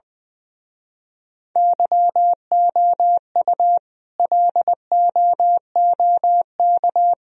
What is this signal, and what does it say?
Morse code, heard as a keyed tone, about 22 wpm, 700 Hz. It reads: YOU LOOK
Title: Speed-Racing - Sets of 4 Words 20wpm